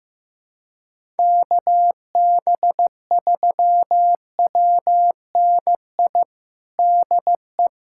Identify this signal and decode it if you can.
Morse code, heard as a keyed tone, about 15 wpm, 700 Hz. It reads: KB3WNI DE
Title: USA Callsign DE Callsign 15wpm